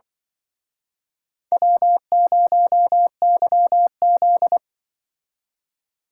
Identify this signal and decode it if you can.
Morse code, heard as a keyed tone, about 25 wpm, 700 Hz. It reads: W0YZ
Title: Speed Racing - USA 1x2 Call Signs 20wpm